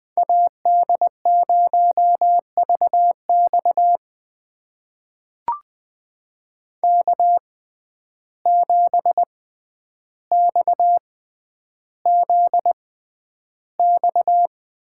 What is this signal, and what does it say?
Morse code, heard as a keyed tone, about 20 wpm, spaced as for 10 wpm, 700 Hz. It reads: AD0VX K7XZX
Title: USA Call Signs - 5 Characters - 6x Character Spacing - 20wpm